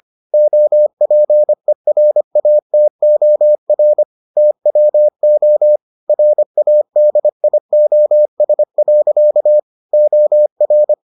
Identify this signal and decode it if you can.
Morse code, heard as a keyed tone, about 25 wpm, 600 Hz. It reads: OPERATOR TWO RADIOS. OR